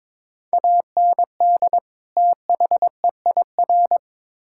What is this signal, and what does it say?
Morse code, heard as a keyed tone, about 22 wpm, 700 Hz. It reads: AND THEIR
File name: Build A Phrase - 2-Word Phrases - 22wpm